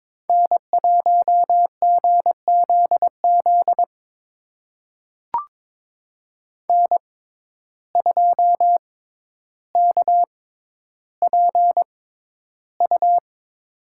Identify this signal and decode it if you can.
Morse code, heard as a keyed tone, about 22 wpm, spaced as for 11 wpm, 700 Hz. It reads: N1GZZ N2KPU
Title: USA Call Signs - 5 Characters - 6x Character Spacing - 22wpm